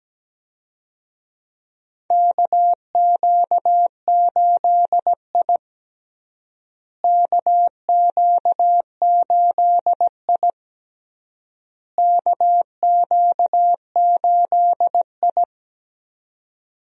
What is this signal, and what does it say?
Morse code, heard as a keyed tone, about 17 wpm, 700 Hz. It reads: KQ8I KQ8I KQ8I
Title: USA Call Signs - 4 Characters - Repeated 3x - 17wpm